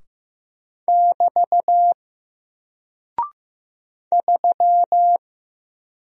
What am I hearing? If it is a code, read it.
Morse code, heard as a keyed tone, about 15 wpm, 700 Hz. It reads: <BT> 3
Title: Single Character (Letter:Number:Punctuation:Prosign) Alt 15wpm